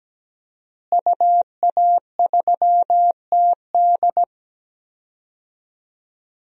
Morse code, heard as a keyed tone, about 17 wpm, 700 Hz. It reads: UA3TD
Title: DXCC-100 Call Signs - 5 Characters - Repeated 3x - 17wpm